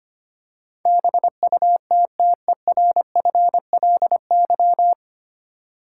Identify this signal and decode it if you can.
Morse code, heard as a keyed tone, about 25 wpm, 690 Hz. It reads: BUTTERFLY
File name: practice / 25wpm / Easy Words with Hints - 25wpm